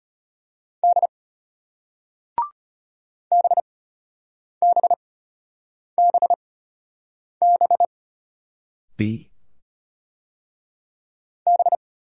Morse code, heard as a keyed tone, about 32 wpm, spaced as for 8 wpm, 700 Hz. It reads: D BBBB B